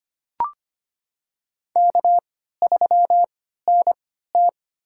Morse code, heard as a keyed tone, about 25 wpm, spaced as for 14 wpm, 700 Hz. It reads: K3NT
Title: USA Call Signs - 3x Character Spacing - 25wpm